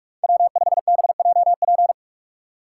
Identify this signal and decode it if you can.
Morse code, heard as a keyed tone, about 41 wpm, 700 Hz. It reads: W5BJP